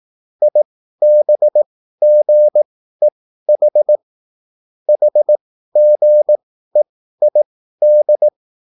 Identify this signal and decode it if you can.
Morse code, heard as a keyed tone, about 18 wpm, spaced as for 13 wpm, 600 Hz. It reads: IBGEH HGEID
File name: QSL / CW lesson